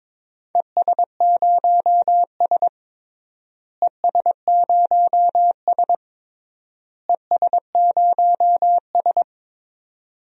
Morse code, heard as a keyed tone, about 22 wpm, 700 Hz. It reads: ES0S ES0S ES0S
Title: DXCC-100 Call Signs - 4 Characters - Repeated 3x - 22wpm